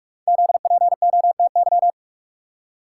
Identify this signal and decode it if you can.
Morse code, heard as a keyed tone, about 44 wpm, 700 Hz. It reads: ZPOTY
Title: Code Groups - 5 Letters - 45wpm